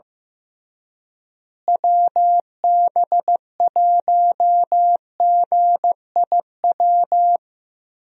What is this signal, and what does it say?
Morse code, heard as a keyed tone, about 15 wpm, 700 Hz. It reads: WB1GIW
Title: USA Call Signs - 6 Characters - Repeated 3x - 15wpm